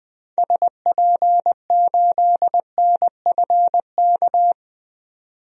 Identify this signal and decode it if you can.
Morse code, heard as a keyed tone, about 21 wpm, 700 Hz. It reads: SP8NFK
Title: DXCC-100 Call Signs - 6 Characters - 20wpm